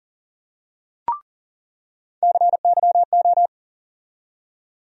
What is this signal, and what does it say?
Morse code, heard as a keyed tone, about 40 wpm, 700 Hz. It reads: CYO
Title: USA Call Sign Postfixes - 3 Letters 40wpm